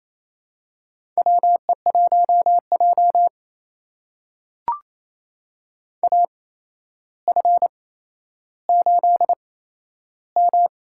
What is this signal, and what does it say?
Morse code, heard as a keyed tone, about 28 wpm, spaced as for 12 wpm, 700 Hz. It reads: WE1J AF8M
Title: USA Call Signs - 4 Characters - 8x Character Spacing - 28wpm